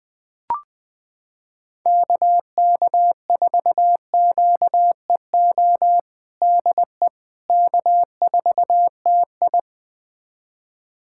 Morse code, heard as a keyed tone, about 20 wpm, 700 Hz. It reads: KK4QEO DE K4TI
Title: USA Callsign DE Callsign 20wpm